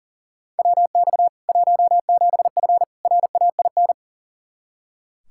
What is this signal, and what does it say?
Morse code, heard as a keyed tone, about 40 wpm, 700 Hz. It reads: WX 17F RAIN
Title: K1USN SST QSO Practice 40wpm